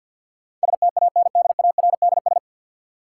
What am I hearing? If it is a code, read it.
Morse code, heard as a keyed tone, about 50 wpm, 700 Hz. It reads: STANDARDS
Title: Sentences from Top 1000 Words - Review Part 6 - New Word Drill - 50wpm